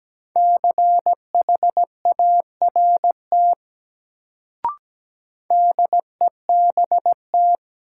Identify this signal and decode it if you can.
Morse code, heard as a keyed tone, about 17 wpm, 700 Hz. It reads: CHART DEBT
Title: Sentences from Top 1300 Words - New Word Drill - 17wpm